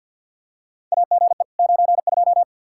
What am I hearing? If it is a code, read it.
Morse code, heard as a keyed tone, about 49 wpm, 700 Hz. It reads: AGE 92
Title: Practice QSO - Exchange 3 - More Info - 50wpm